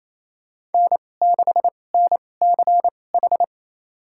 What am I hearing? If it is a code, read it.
Morse code, heard as a keyed tone, about 28 wpm, spaced as for 20 wpm, 700 Hz. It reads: N6NCH
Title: USA Call Signs - 5 Characters - 2x Character Spacing - 28wpm